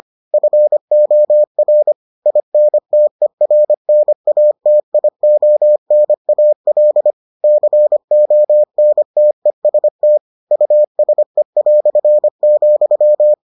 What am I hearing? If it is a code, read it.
Morse code, heard as a keyed tone, about 25 wpm, 600 Hz. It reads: FOR INTERNATIONAL CONTEST USE",